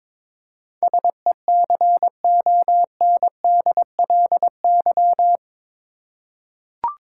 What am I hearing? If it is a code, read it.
Morse code, heard as a keyed tone, about 22 wpm, 700 Hz. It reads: SECONDLY
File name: practice / 22wpm / Eight Letter Words - 22wpm